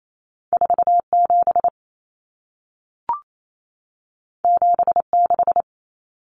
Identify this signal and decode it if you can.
Morse code, heard as a keyed tone, about 28 wpm, 700 Hz. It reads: 47 76
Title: Numbers - 2-Digit 28wpm